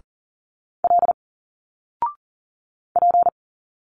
Morse code, heard as a keyed tone, about 40 wpm, 700 Hz. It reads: L P
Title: Single Letters 40wpm